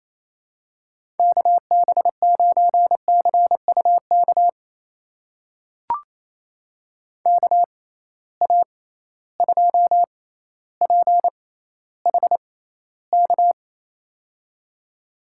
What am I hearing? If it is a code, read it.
Morse code, heard as a keyed tone, about 28 wpm, spaced as for 14 wpm, 700 Hz. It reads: KB9CUK KA2PHK
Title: USA Call Signs - 6 Characters - 6x Character Spacing - 28wpm